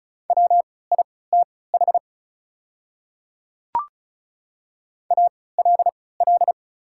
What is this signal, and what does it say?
Morse code, heard as a keyed tone, about 36 wpm, spaced as for 20 wpm, 690 Hz. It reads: WITH ALL
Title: 3x Character Spacing - Top 100 Words - 35wpm